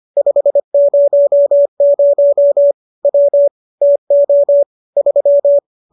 Morse code, heard as a keyed tone, about 25 wpm, 570 Hz. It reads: 500 W TO 3